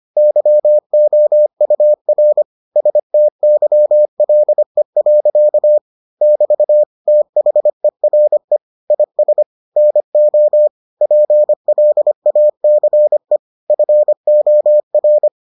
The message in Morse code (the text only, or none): YOUR STYLE. <BT> THERE IS NO PLACE FOR